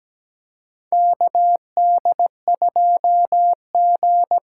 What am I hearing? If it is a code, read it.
Morse code, heard as a keyed tone, about 17 wpm, 700 Hz. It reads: KD2G
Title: USA Call Signs - 6 Characters - 8x Character Spacing - 17wpm